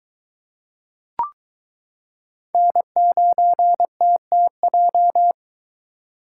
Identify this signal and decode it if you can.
Morse code, heard as a keyed tone, about 22 wpm, 700 Hz. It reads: N9TTJ